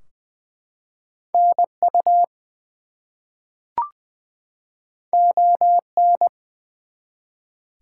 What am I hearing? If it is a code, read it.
Morse code, heard as a keyed tone, about 20 wpm, 700 Hz. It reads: NU ON
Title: US States and Canadian Provinces 20wpm